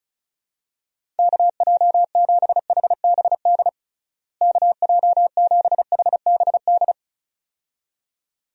Morse code, heard as a keyed tone, about 36 wpm, 700 Hz. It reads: KJ7HBD KJ7HBD
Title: USA Call Signs - 6 Characters - Repeated 2x - 35wpm